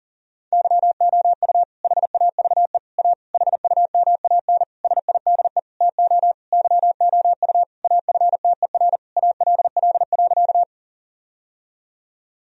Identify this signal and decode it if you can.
Morse code, heard as a keyed tone, about 40 wpm, 700 Hz. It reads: YOU HAVE A HUMAN SIDE TO YOU AFTER ALL.
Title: Sentences from Top 1000 Words - Review Part 3 - 40wpm